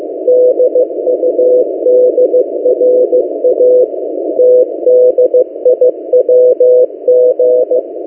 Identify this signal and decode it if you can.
Morse code, heard as a keyed tone, about 15 wpm, 500 Hz. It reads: DUDRA TDIWG
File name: M08a- -1355utc-8096khz